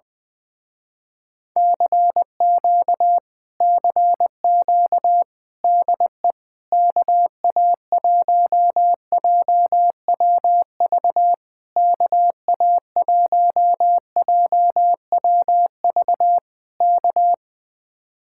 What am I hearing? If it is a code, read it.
Morse code, heard as a keyed tone, about 20 wpm, 700 Hz. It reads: CQ CQ DE KA1JWV KA1JWV K
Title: Speed-Racing Common QSO Phrases Encore - 17wpm